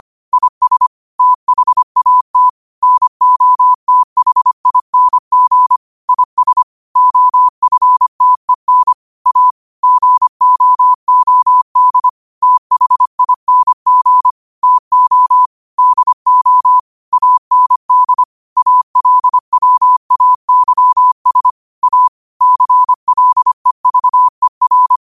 Morse code, heard as a keyed tone, about 25 wpm, 1000 Hz. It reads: IS THAT NOTHING IS OFTEN A GOOD THING TO DO AND ALWAYS A CLEVER
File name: qotdfiles / 25wpm